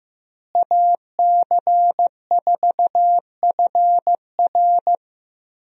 Morse code, heard as a keyed tone, about 15 wpm, 700 Hz. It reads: AC4FR
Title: USA Call Signs - 5 Characters - Repeated 2x - 15wpm